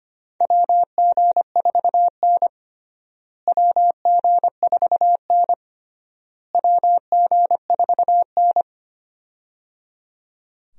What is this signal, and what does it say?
Morse code, heard as a keyed tone, about 25 wpm, 700 Hz. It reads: WG4N WG4N WG4N
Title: USA Call Signs - 4 Characters - Repeated 3x - 25wpm